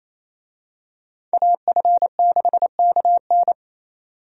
Morse code, heard as a keyed tone, about 28 wpm, 700 Hz. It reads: AF6KN
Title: USA Call Signs - 5 Characters - 28wpm